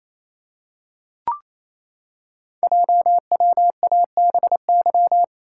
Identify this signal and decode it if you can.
Morse code, heard as a keyed tone, about 27 wpm, 700 Hz. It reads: JWABY
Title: Code Groups - 5 Letters - 28wpm